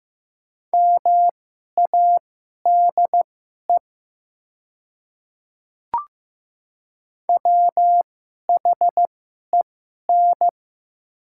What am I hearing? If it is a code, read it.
Morse code, heard as a keyed tone, about 15 wpm, spaced as for 11 wpm, 700 Hz. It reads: MADE WHEN
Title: 2x Character Spacing - Top 200 Words - 15wpm